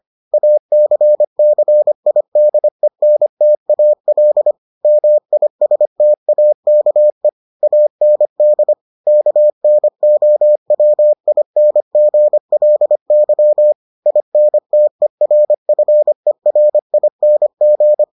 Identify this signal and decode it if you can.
Morse code, heard as a keyed tone, about 25 wpm, 600 Hz. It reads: ACCIDENTAL MISTAKE AND KNOWINGLY INTERFERING